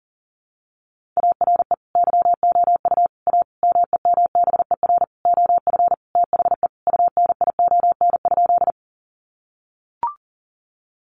Morse code, heard as a keyed tone, about 40 wpm, 700 Hz. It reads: ARE YOU A MEMBER OF THE UNION?
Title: Sentences from Top 1400 Words 40wpm